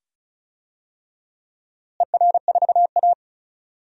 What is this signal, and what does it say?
Morse code, heard as a keyed tone, about 36 wpm, 700 Hz. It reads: ER4A
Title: DXCC-100 Call Signs - 4 Characters - Repeated 2x - 35wpm